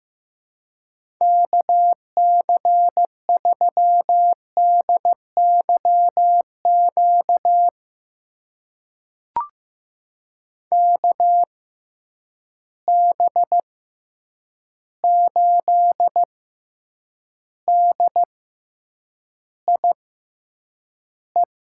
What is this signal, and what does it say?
Morse code, heard as a keyed tone, about 15 wpm, spaced as for 8 wpm, 700 Hz. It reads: KC3DYQ KB8DIE
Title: USA Call Signs - 6x Character Spacing - 15wpm